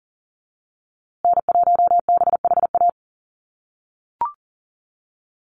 Morse code, heard as a keyed tone, about 40 wpm, 700 Hz. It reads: N1BHA
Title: USA Call Signs Repeated 3x 40wpm